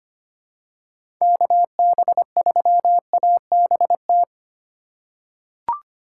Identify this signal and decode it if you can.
Morse code, heard as a keyed tone, about 25 wpm, 700 Hz. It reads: KB3ABT
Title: USA Call Signs - 4x Character Spacing - 25wpm